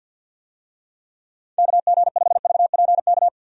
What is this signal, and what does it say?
Morse code, heard as a keyed tone, about 53 wpm, 700 Hz. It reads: KG5VPK
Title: USA Call Signs - 6 Characters - 50wpm